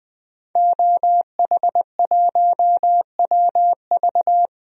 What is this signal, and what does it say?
Morse code, heard as a keyed tone, about 20 wpm, 700 Hz. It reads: OH1WV
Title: DXCC-100 Call Signs - 5 Characters - 20wpm